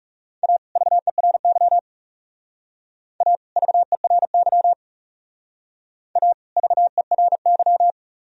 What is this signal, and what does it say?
Morse code, heard as a keyed tone, about 38 wpm, 700 Hz. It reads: A VERY A VERY A VERY